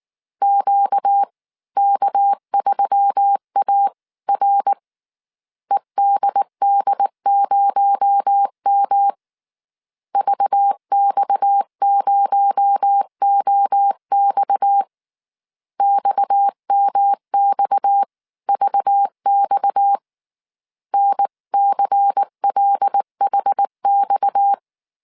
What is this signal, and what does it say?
Morse code, heard as a keyed tone, about 19 wpm, 800 Hz. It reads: QK3AR EDD0M VX0OX XMXVX NCLHX
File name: CW / DET / 1 6 1500C S C 18wpm 2356